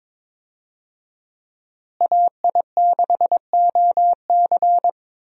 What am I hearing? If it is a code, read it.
Morse code, heard as a keyed tone, about 22 wpm, 700 Hz. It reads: AI6OC
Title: USA Call Signs - 5 Characters - Repeated 2x - 22wpm